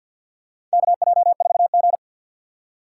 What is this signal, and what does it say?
Morse code, heard as a keyed tone, about 47 wpm, 700 Hz. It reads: KJ4G
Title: USA Call Signs - 4 Characters - 3x Character Spacing - 50wpm